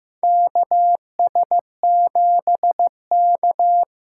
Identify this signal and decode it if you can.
Morse code, heard as a keyed tone, about 15 wpm, 700 Hz. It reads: KS7K
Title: USA Call Signs - 4 Characters - 6x Character Spacing - 15wpm